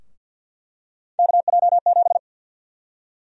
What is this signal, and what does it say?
Morse code, heard as a keyed tone, about 51 wpm, 700 Hz. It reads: KJ7
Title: USA Call Sign Prefixes - Repeated 2x - 50wpm